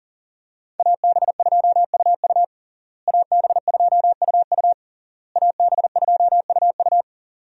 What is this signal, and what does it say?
Morse code, heard as a keyed tone, about 40 wpm, 700 Hz. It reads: AB2UU AB2UU AB2UU
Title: USA Call Signs - 5 Characters - Repeated 3x - 40wpm